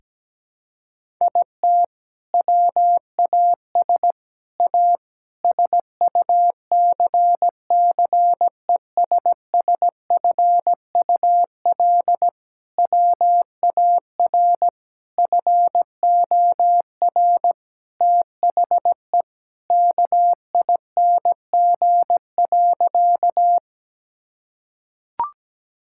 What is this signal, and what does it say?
Morse code, heard as a keyed tone, about 17 wpm, 700 Hz. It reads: IT WAS A SUCCESSFUL WAR FOR THE KING.
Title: Sentences from Top 1000 Words - Review Part 6 - 17wpm